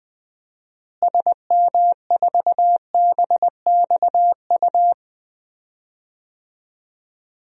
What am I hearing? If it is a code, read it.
Morse code, heard as a keyed tone, about 21 wpm, 700 Hz. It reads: SM4BXU
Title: DXCC-100 Call Signs - 6 Characters - Repeated 3x - 20wpm